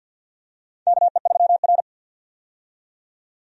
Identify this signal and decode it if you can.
Morse code, heard as a keyed tone, about 51 wpm, 700 Hz. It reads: KE3R
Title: USA Call Signs - 4 Characters - 50wpm